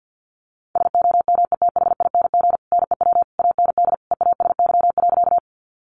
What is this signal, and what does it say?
Morse code, heard as a keyed tone, about 50 wpm, 700 Hz. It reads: SOMETHING NEW AND EASY.
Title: Sentences from Top 700 Words 50wpm